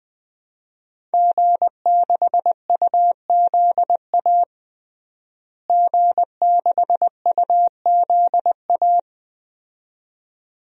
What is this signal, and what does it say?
Morse code, heard as a keyed tone, about 20 wpm, 700 Hz. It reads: G6UZA G6UZA
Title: DXCC-100 Call Signs - 5 Characters - Repeated 2x - 20wpm